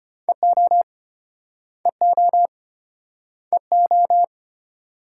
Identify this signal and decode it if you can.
Morse code, heard as a keyed tone, about 32 wpm, 700 Hz. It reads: EO EO EO